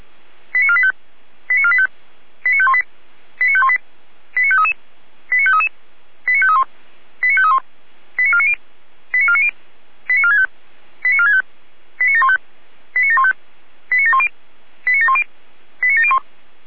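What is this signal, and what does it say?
Five-tone selective call (ZVEI), heard as quick runs of five tones, tones of about 70 ms. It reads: 89485 89485 89518 89518 89533 89533 89531 89531 89490 89490 89465 89465 89715 89715 89710 89710 89701
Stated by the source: ZVEI five-tone call 89011 received on an SDR.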